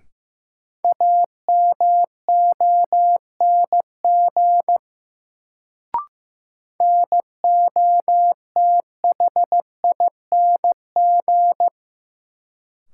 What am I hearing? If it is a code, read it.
Morse code, heard as a keyed tone, about 15 wpm, 700 Hz. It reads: AMONG NOTHING
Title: Sentences from Top 500 Words - New Word Drill - 15wpm